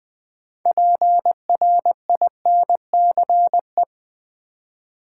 Morse code, heard as a keyed tone, about 20 wpm, 700 Hz. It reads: PRINCE
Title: 6 Letter Words - First 3 Letters Given - 20wpm